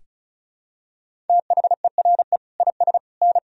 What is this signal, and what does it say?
Morse code, heard as a keyed tone, about 36 wpm, 700 Hz. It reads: THERE IS N
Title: Sentences from Top 200 Words and Abbreviations 35wpm